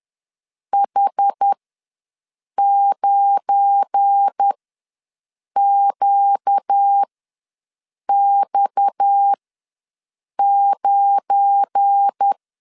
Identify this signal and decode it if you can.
Morse code, heard as a keyed tone, about 10 wpm, spaced as for 6 wpm, 800 Hz. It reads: H9QX9
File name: LEZIONI / 7 GF9Z / 7 GF9 ws 10wpm 4034